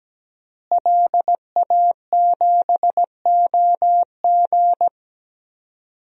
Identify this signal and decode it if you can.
Morse code, heard as a keyed tone, about 17 wpm, 700 Hz. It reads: LA7OG
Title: DXCC-100 Call Signs - 5 Characters - Repeated 3x - 17wpm